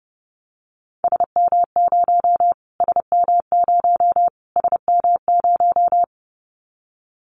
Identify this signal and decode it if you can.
Morse code, heard as a keyed tone, about 30 wpm, 700 Hz. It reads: SM0 SM0 SM0
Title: Callsign Prefixes for Top 25 DXCC - Repeated 3x - 30wpm